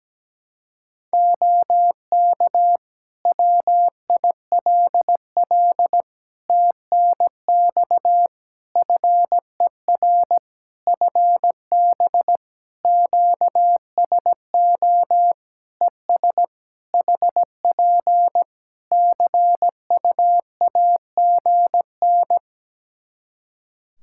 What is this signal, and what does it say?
Morse code, heard as a keyed tone, about 17 wpm, 700 Hz. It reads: OK WILL TNX FER FB QSO ES HP CUAGN
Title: Practice QSO - Exchange 4 - Ending - 17wpm